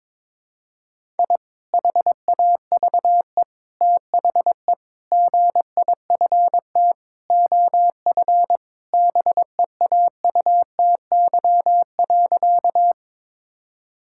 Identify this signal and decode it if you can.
Morse code, heard as a keyed tone, about 22 wpm, 700 Hz. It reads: I HAVE THE GIFT OF BEAUTY.
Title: Sentences from Top 1000 Words - Review Part 6 - 22wpm